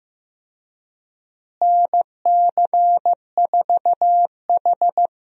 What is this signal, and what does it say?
Morse code, heard as a keyed tone, about 16 wpm, 700 Hz. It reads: NC4H
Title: USA Call Signs - 4 Characters - Repeated 3x - 15wpm